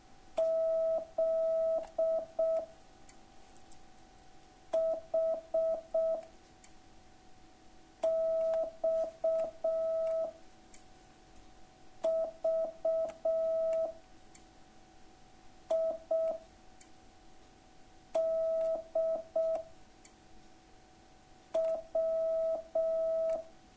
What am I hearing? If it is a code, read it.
Morse code, heard as a keyed tone, about 6 wpm, 650 Hz. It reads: Z H X V I D W